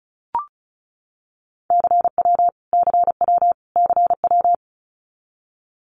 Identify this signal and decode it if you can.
Morse code, heard as a keyed tone, about 34 wpm, 700 Hz. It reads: CW CW CW
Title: Abbreviations Repeated 3x - 35wpm